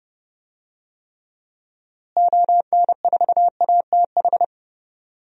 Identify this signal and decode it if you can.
Morse code, heard as a keyed tone, about 30 wpm, 700 Hz. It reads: ON4ATH
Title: DXCC-100 Call Signs - 6 Characters - Repeated 2x - 30wpm